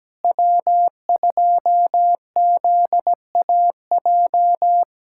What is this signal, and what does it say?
Morse code, heard as a keyed tone, about 17 wpm, 700 Hz. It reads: W2ZAJ
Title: USA Callsign DE Callsign - 5 Characters - 17wpm